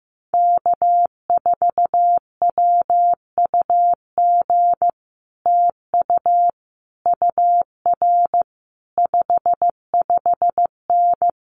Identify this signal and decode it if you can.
Morse code, heard as a keyed tone, about 15 wpm, 700 Hz. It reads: K4WUG TU UR 55N
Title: Common QSO Phrases Encore 15wpm